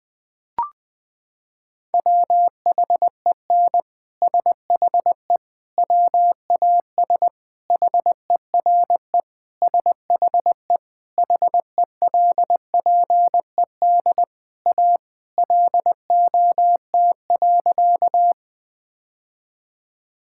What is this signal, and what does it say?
Morse code, heard as a keyed tone, about 20 wpm, 700 Hz. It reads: WHEN SHE WAS HERE SHE HELPED A LOT.
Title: Sentences with 3-Word Phrases - 20wpm